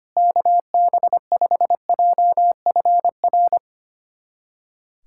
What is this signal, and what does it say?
Morse code, heard as a keyed tone, about 25 wpm, 700 Hz. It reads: KB5JFR
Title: USA Callsign DE Callsign - 6 Characters - 25wpm